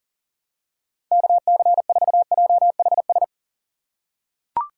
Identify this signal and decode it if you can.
Morse code, heard as a keyed tone, about 40 wpm, 700 Hz. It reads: KC4JHS
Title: USA Call Signs - 6 Characters - 2x Character Spacing - 40wpm